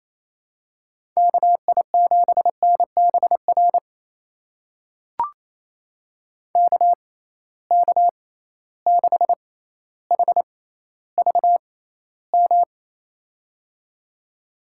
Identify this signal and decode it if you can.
Morse code, heard as a keyed tone, about 28 wpm, spaced as for 14 wpm, 700 Hz. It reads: KI7NBR KK6HVM
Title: USA Call Signs - 6 Characters - 6x Character Spacing - 28wpm